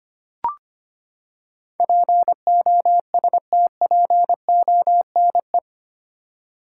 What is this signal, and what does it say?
Morse code, heard as a keyed tone, about 25 wpm, 700 Hz. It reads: POSTPONE
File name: Eight Letter Words - 25wpm